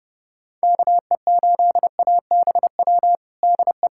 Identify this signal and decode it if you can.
Morse code, heard as a keyed tone, about 30 wpm, 700 Hz. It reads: KE8ABW DE
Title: USA Callsign DE Callsign - 6 Characters - 30wpm